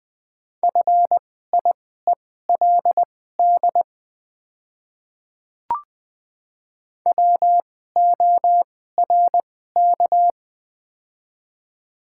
Morse code, heard as a keyed tone, about 20 wpm, spaced as for 14 wpm, 700 Hz. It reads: FIELD WORK
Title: 2x Character Spacing - Top 500 Words - 20wpm